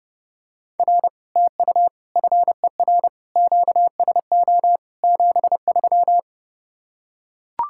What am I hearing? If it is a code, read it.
Morse code, heard as a keyed tone, about 30 wpm, 700 Hz. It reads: R TU FER QSO 73